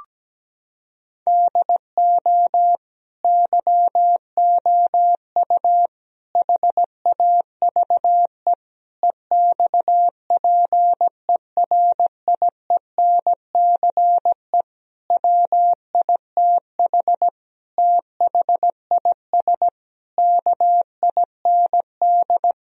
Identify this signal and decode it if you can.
Morse code, heard as a keyed tone, about 17 wpm, 700 Hz. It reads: DO YOU HAVE EXPERIENCE WITH THIS KIND
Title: Sentences from Top 1000 Words - Review Part 3 - 17wpm